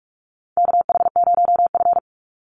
Morse code, heard as a keyed tone, about 44 wpm, 700 Hz. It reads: KH0F
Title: USA Call Signs 45wpm